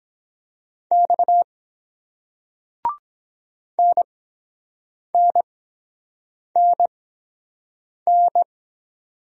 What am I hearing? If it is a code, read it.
Morse code, heard as a keyed tone, about 23 wpm, spaced as for 7 wpm, 700 Hz. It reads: X NNNN